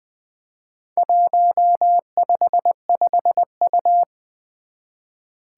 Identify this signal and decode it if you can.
Morse code, heard as a keyed tone, about 20 wpm, 700 Hz. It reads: 155U
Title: Four Challenging Letter-Number Combinations 20wpm